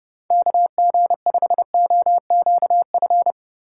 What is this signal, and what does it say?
Morse code, heard as a keyed tone, about 30 wpm, 700 Hz. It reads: KG5OQF
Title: USA Call Signs - 6 Characters - 4x Character Spacing - 30wpm